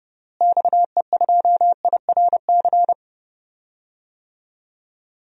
Morse code, heard as a keyed tone, about 31 wpm, 700 Hz. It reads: XE2IRC
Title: DXCC-100 Call Signs - 6 Characters - Repeated 3x - 30wpm